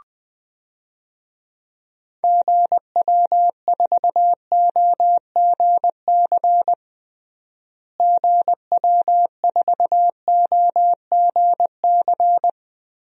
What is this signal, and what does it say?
Morse code, heard as a keyed tone, about 20 wpm, 700 Hz. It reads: GW4OGC GW4OGC
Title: DXCC-100 Call Signs - 6 Characters - Repeated 3x - 20wpm